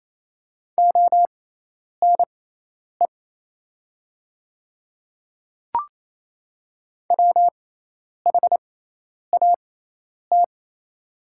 Morse code, heard as a keyed tone, about 28 wpm, spaced as for 10 wpm, 700 Hz. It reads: ONE WHAT
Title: 6x Character Spacing - Top 100 Words - 28wpm